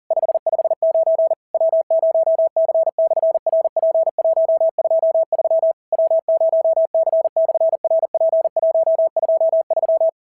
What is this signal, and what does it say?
Morse code, heard as a keyed tone, about 40 wpm, 650 Hz. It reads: 559 W0C/RP123 W0C/RP123
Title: SOTA QSO Practice 40wpm